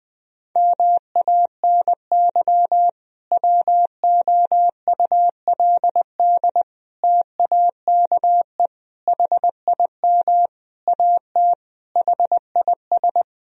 Morse code, heard as a keyed tone, about 20 wpm, 700 Hz. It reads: MANY WOULD TAKE HIM AT HIS
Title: Sentences from Top 1000 Words - Review Part 1 - 20wpm